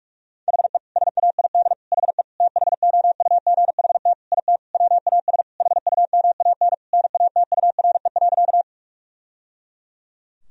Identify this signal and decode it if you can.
Morse code, heard as a keyed tone, about 41 wpm, 700 Hz. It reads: HE SAID HE THOUGHT IT WAS HUMAN NATURE.
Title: Sentences from Top 1000 Words - Review Part 3 - 45wpm